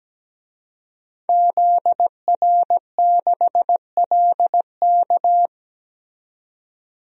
Morse code, heard as a keyed tone, about 17 wpm, 700 Hz. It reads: ZR6LK
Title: DXCC-100 Call Signs - 5 Characters - 17wpm